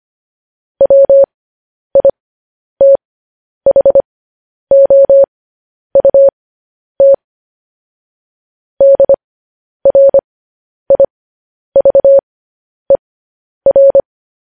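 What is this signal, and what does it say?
Morse code, heard as a keyed tone, about 25 wpm, spaced as for 10 wpm, 550 Hz. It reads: WITHOUT DRIVER
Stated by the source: Morse code news headlines practice, at 10 WPM.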